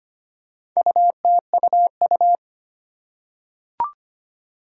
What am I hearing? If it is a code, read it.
Morse code, heard as a keyed tone, about 25 wpm, 700 Hz. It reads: UTUU
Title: Four Challenging Letter-Number Combinations 25wpm